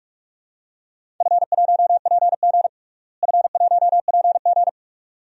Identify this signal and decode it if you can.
Morse code, heard as a keyed tone, about 46 wpm, 700 Hz. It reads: F1PG F1PG
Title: DXCC-100 Call Signs - 4 Characters - Repeated 2x - 45wpm